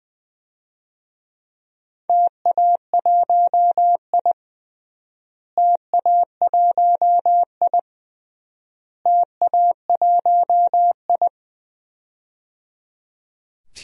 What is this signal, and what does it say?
Morse code, heard as a keyed tone, about 21 wpm, 690 Hz. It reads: TA1I TA1I TA1I
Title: DXCC-100 Call Signs - 4 Characters - Repeated 3x - 20wpm